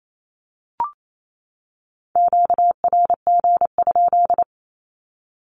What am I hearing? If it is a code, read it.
Morse code, heard as a keyed tone, about 28 wpm, 700 Hz. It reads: QRG?
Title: Q-Codes 28wpm